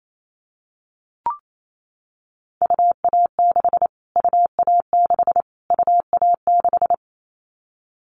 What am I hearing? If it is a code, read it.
Morse code, heard as a keyed tone, about 28 wpm, 700 Hz. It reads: UA6 UA6 UA6
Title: Callsign Prefixes for Top 25 DXCC - Repeated 3x - 28wpm